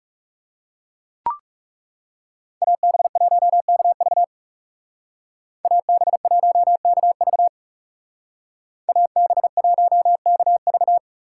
Morse code, heard as a keyed tone, about 37 wpm, 700 Hz. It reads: AB1KV AB1KV AB1KV
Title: Speed Racing - USA Call Signs 30wpm